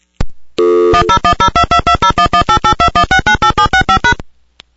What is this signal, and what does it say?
DTMF, heard as telephone touch tones, tones of about 90 ms. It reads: 10102220118821698765#
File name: thuraya inc